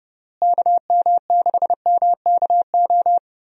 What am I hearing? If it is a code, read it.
Morse code, heard as a keyed tone, about 32 wpm, 700 Hz. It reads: KM6MKO
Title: USA Call Signs - 6 Characters - Repeated 3x - 30wpm